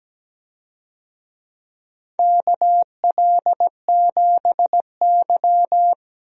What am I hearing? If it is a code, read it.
Morse code, heard as a keyed tone, about 17 wpm, 700 Hz. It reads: KL7Y
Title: DXCC-100 Call Signs - 4 Characters - 17wpm